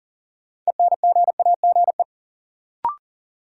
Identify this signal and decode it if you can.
Morse code, heard as a keyed tone, about 40 wpm, 700 Hz. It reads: ENGAGE
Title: Sentences from Top 1500 Words - New Word Drill Alt - 40wpm